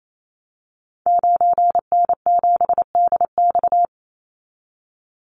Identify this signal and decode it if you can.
Morse code, heard as a keyed tone, about 28 wpm, 700 Hz. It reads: 9N7DX
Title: Call Signs Repeated 3x - 28wpm